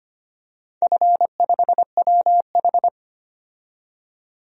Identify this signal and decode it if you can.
Morse code, heard as a keyed tone, about 25 wpm, 700 Hz. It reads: F5WH
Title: DXCC-100 Call Signs - 4 Characters - 25wpm